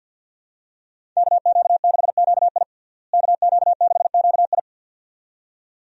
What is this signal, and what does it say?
Morse code, heard as a keyed tone, about 47 wpm, 700 Hz. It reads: KQ6QI KQ6QI
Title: USA Call Signs - 5 Characters - Repeated 2x - 50wpm